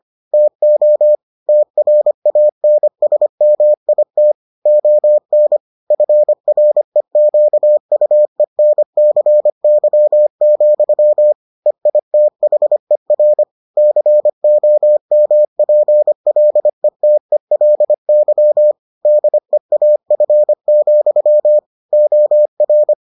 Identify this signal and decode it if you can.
Morse code, heard as a keyed tone, about 25 wpm, 600 Hz. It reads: TO TRANSMIT ON FREQUENCY, EITHER COMPLETELY DEAF, OR